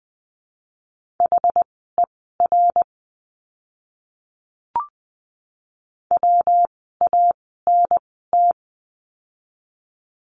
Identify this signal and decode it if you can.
Morse code, heard as a keyed tone, about 20 wpm, spaced as for 14 wpm, 700 Hz. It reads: HER WANT
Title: 115 Lesson 19f - Letter w - Words 20wpm